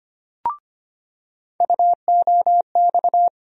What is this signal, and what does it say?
Morse code, heard as a keyed tone, about 25 wpm, 700 Hz. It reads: UOX
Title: USA Call Sign Postfixes - 3 Letters 25wpm